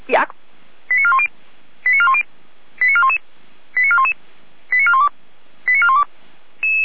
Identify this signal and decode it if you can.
Five-tone selective call (ZVEI), heard as quick runs of five tones, tones of about 70 ms, with speech. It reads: 89410 89410 89411 89411 89412 89412